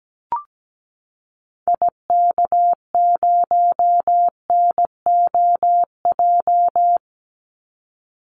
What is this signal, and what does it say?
Morse code, heard as a keyed tone, about 17 wpm, 700 Hz. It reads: IK0NOJ
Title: Call Signs Repeated 2x - 17wpm